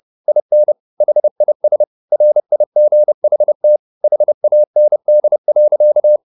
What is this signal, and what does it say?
Morse code, heard as a keyed tone, about 30 wpm, 600 Hz. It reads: IN HIS RIGHT HAND.